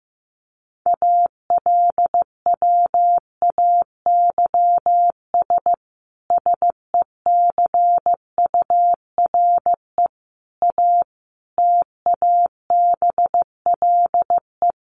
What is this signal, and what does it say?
Morse code, heard as a keyed tone, about 14 wpm, 700 Hz. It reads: ALWAYS SECURE A TABLE
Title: Sentences from Top 1200 Words 15wpm